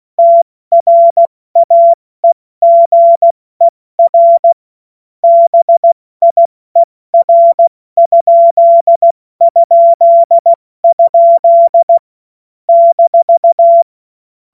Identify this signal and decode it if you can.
Morse code, heard as a keyed tone, about 16 wpm, 680 Hz. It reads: TRAEGER BIER??? -